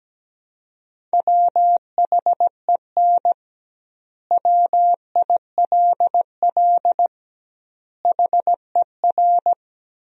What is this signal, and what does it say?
Morse code, heard as a keyed tone, about 17 wpm, 700 Hz. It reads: WHEN WILL HER
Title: 2x Word-Spacing - Sentences from Top 200 Words - 17wpm